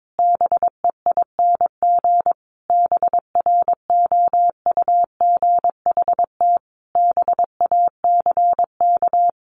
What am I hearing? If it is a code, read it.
Morse code, heard as a keyed tone, about 22 wpm, 700 Hz. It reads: BEING BROUGHT BACK
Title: Sentences from Top 600 Words 22wpm